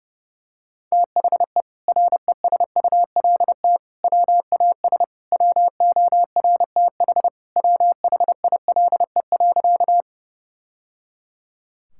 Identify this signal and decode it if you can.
Morse code, heard as a keyed tone, about 30 wpm, 700 Hz. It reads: THE RESULT WAS WORTH WHILE.
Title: Sentences from Top 1000 Words - Review Part 5 - 30wpm